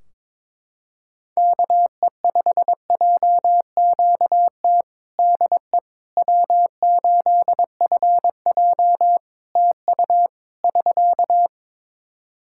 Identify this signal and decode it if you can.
Morse code, heard as a keyed tone, about 22 wpm, 700 Hz. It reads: KE5JQT DE W8FJ TU <SK>
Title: Practice QSO - Exchange 4 - Ending - 22wpm